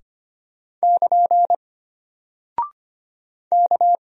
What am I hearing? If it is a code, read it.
Morse code, heard as a keyed tone, about 25 wpm, 700 Hz. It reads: <KN> K
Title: Single Character (Letter:Number:Punctuation:Prosign) Alt 25wpm